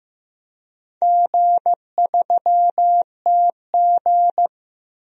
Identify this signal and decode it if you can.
Morse code, heard as a keyed tone, about 15 wpm, 700 Hz. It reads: G3TG
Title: DXCC-100 Call Signs - 4 Characters - Repeated 2x - 15wpm